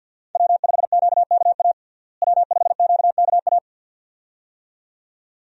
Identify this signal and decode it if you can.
Morse code, heard as a keyed tone, about 52 wpm, 700 Hz. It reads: W5QKA W5QKA
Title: USA Call Signs - 5 Characters - Repeated 2x - 50wpm